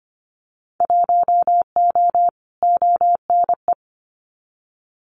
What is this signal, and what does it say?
Morse code, heard as a keyed tone, about 25 wpm, 700 Hz. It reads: 1O ONE
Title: Winter Field Day Exchanges 25wpm